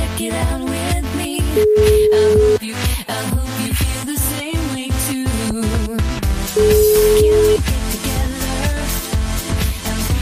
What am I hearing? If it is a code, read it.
Ringback tone (Europe, 425 Hz), with noise.